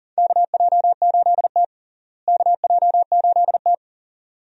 Morse code, heard as a keyed tone, about 40 wpm, 700 Hz. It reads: KJ8T KJ8T
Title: USA Call Signs - 4 Characters - Repeated 2x - 40wpm